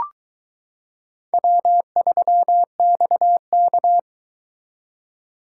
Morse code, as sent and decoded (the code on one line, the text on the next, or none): .-- ...-- -..- -.-
W3XK